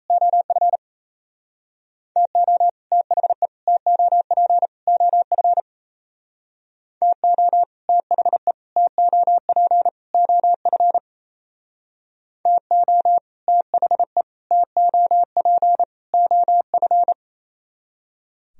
Morse code, heard as a keyed tone, about 34 wpm, 700 Hz. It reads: OF TO THE TOP OF TO THE TOP OF TO THE TOP OF
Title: Speed-Racing - Sets of 4 Words 28wpm